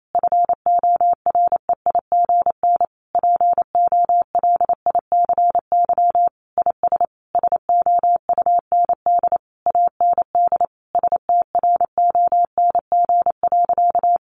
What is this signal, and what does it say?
Morse code, heard as a keyed tone, about 28 wpm, 700 Hz. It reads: FOREIGN POLICY IS SOUND AND STRONG.
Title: Sentences from Top 1400 Words 28wpm